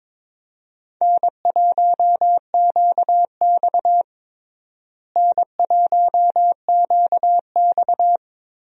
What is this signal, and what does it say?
Morse code, heard as a keyed tone, about 21 wpm, 700 Hz. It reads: N1QX N1QX
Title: USA Call Signs - 4 Characters - Repeated 3x - 22wpm